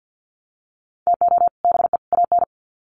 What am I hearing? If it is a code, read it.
Morse code, heard as a keyed tone, about 48 wpm, 700 Hz. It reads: TO BE AN
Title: Sets of 3 Words 50wpm